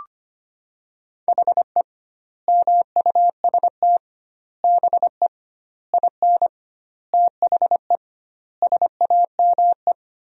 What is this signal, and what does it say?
Morse code, heard as a keyed tone, about 25 wpm, 700 Hz. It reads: HE MUST BE IN THE SAME
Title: 2x Word-Spacing - Sentences from Top 500 Words - 25wpm